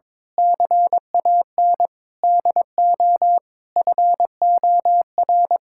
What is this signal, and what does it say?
Morse code, heard as a keyed tone, about 21 wpm, 700 Hz. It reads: CAN DO FOR
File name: Sentences with 9-Word Phrases - 22wpm